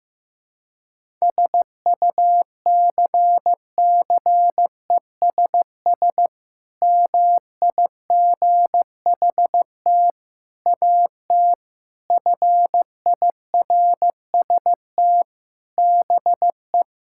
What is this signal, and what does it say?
Morse code, heard as a keyed tone, about 15 wpm, 700 Hz. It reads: SUCCESS MIGHT AT FIRST BE
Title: Sentences from Top 1000 Words - Review Part 6 - 15wpm